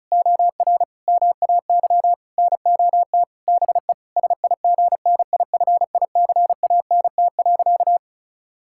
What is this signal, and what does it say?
Morse code, heard as a keyed tone, about 36 wpm, 700 Hz. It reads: OR MAY NOT BE SIGNIFICANT.
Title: Sentences from Top 1000 Words - Review Part 6 - 35wpm